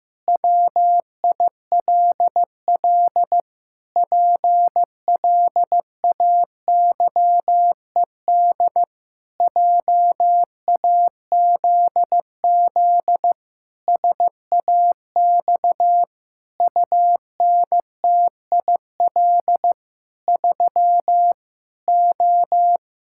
Morse code, heard as a keyed tone, about 15 wpm, 700 Hz. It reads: WILL PLAYED JAZZ SAX UNTIL 3 O
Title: Pangram Challenge v2 - 15wpm